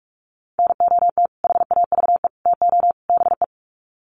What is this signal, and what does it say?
Morse code, heard as a keyed tone, about 44 wpm, 700 Hz. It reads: NOT HAVE TO BE
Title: Sets of 4 Words 45wpm